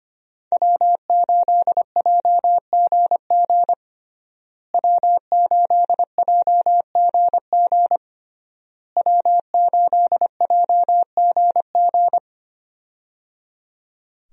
Morse code, heard as a keyed tone, about 25 wpm, 700 Hz. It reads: W8JGG W8JGG W8JGG
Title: USA Call Signs - 5 Characters - Repeated 3x - 25wpm